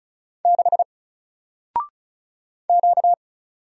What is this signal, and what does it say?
Morse code, heard as a keyed tone, about 34 wpm, 700 Hz. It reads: 6 Q
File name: Single-Letter-Number-35wpm